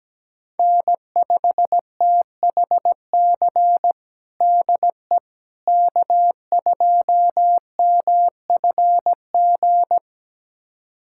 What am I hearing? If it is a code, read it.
Morse code, heard as a keyed tone, about 17 wpm, 700 Hz. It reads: N5THC DE K2MFG
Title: USA Callsign DE Callsign - 5 Characters - 17wpm